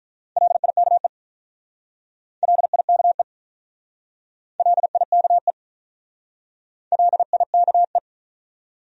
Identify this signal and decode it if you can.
Morse code, heard as a keyed tone, about 46 wpm, 700 Hz. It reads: LIKE LIKE LIKE LIKE